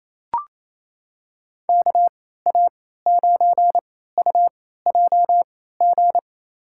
Morse code, heard as a keyed tone, about 28 wpm, spaced as for 16 wpm, 700 Hz. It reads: KA9UJG
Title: USA Call Signs - 6 Characters - 3x Character Spacing - 28wpm